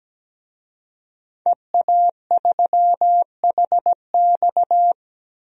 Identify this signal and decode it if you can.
Morse code, heard as a keyed tone, about 17 wpm, 700 Hz. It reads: EA3HX
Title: DXCC-100 Call Signs - 5 Characters - Repeated 3x - 17wpm